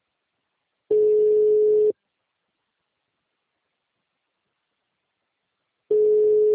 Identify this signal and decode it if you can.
Ringback tone (Europe, 425 Hz).
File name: out- -2048